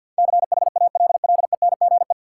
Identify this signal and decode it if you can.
Morse code, heard as a keyed tone, about 49 wpm, 700 Hz. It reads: CHALLENGE
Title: Sentences from Top 1000 Words - Review Part 7 - New Word Drill - 50wpm